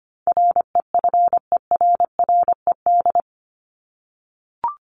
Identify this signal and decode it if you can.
Morse code, heard as a keyed tone, about 25 wpm, 700 Hz. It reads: REFERRED
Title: ED Suffix - Top 100 Words with Base Words 25wpm